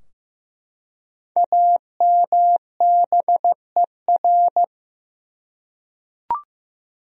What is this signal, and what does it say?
Morse code, heard as a keyed tone, about 15 wpm, 700 Hz. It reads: AMBER
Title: USA Common Names - Max 5 Letters - 15wpm